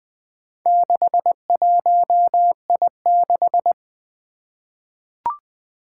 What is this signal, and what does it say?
Morse code, heard as a keyed tone, about 21 wpm, 700 Hz. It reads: 61I6
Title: Four Challenging Letter-Number Combinations 20wpm